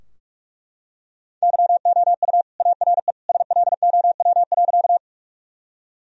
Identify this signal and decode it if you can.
Morse code, heard as a keyed tone, about 41 wpm, 700 Hz. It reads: YOU ARE SLOW.
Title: Sentences from Top 1000 Words - Review Part 2 - 45wpm